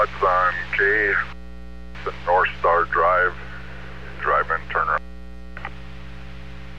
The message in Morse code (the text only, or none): none